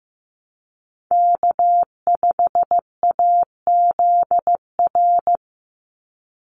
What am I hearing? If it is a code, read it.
Morse code, heard as a keyed tone, about 15 wpm, 700 Hz. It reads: K5AZR
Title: Call Signs Repeated 2x - 15wpm